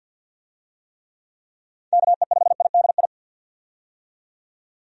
Morse code, heard as a keyed tone, about 53 wpm, 700 Hz. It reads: KE5IDI
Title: USA Call Signs - 6 Characters - 50wpm